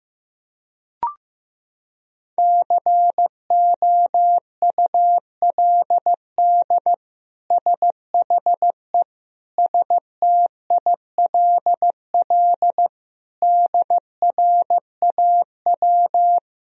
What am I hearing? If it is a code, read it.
Morse code, heard as a keyed tone, about 15 wpm, 700 Hz. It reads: COULD SHE STILL DRAW
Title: Sentences from Top 1000 Words - Review Part 1 - 15wpm